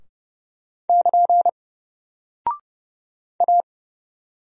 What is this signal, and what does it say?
Morse code, heard as a keyed tone, about 30 wpm, 700 Hz. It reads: <KN> A
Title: Single Character (Letter:Number:Punctuation:Prosign) Alt 30wpm